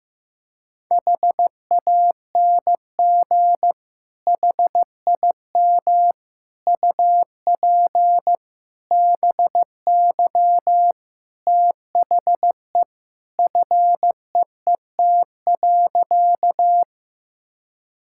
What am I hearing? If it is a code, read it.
Morse code, heard as a keyed tone, about 15 wpm, 700 Hz. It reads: HANG HIM UP BY THE FEET.
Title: Sentences from Top 1600 Words 15wpm